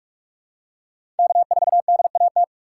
Morse code, heard as a keyed tone, about 46 wpm, 700 Hz. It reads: K4DAT
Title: USA Call Signs - 5 Characters - 3x Character Spacing - 45wpm